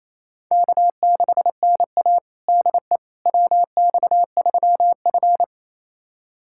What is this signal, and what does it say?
Morse code, heard as a keyed tone, about 28 wpm, 700 Hz. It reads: K6NA DE WX3F
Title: USA Callsign DE Callsign - 4 Characters - 28wpm